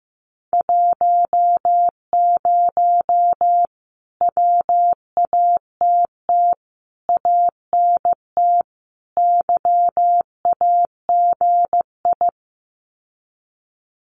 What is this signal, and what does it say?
Morse code, heard as a keyed tone, about 15 wpm, 700 Hz. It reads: 10 WATT ANT YAGI
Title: Common QSO Phrases Encore 15wpm